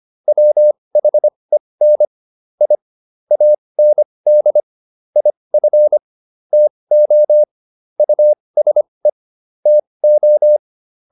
Morse code, heard as a keyed tone, about 25 wpm, spaced as for 20 wpm, 600 Hz. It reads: WHEN I AND IF TO USE TO